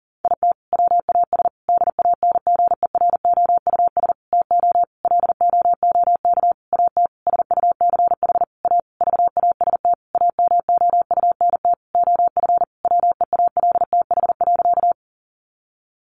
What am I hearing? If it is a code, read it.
Morse code, heard as a keyed tone, about 37 wpm, 700 Hz. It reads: IT WAS DANGEROUS TO LOOK AT SUCH A VAST AMOUNT OF WEALTH.
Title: Sentences from Top 1300 Words 40wpm